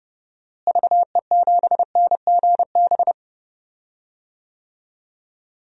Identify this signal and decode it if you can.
Morse code, heard as a keyed tone, about 32 wpm, 700 Hz. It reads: VE7NGB
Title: DXCC-100 Call Signs - 6 Characters - Repeated 2x - 30wpm